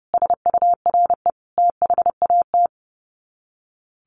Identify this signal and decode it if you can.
Morse code, heard as a keyed tone, about 28 wpm, 700 Hz. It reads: SURE THAT
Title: Sets of 5 Words 30wpm